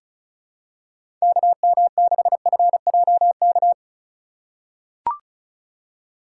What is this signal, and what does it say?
Morse code, heard as a keyed tone, about 36 wpm, 700 Hz. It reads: KM6FJK
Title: USA Call Signs - 6 Characters - 4x Character Spacing - 35wpm